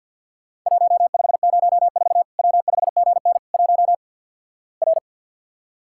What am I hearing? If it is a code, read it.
Morse code, heard as a keyed tone, about 51 wpm, 700 Hz. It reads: 1504 W5GN 1 R
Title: Worked All Europe DX Contest - QTC Exchange - 50wpm